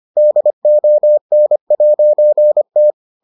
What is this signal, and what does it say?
Morse code, heard as a keyed tone, about 25 wpm, 600 Hz. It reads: DON'T